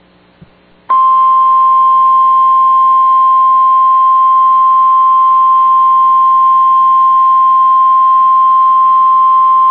Emergency Alert System (SAME) tones (the 1050 Hz attention tone). It is heard from 1.2 to 9.2 s.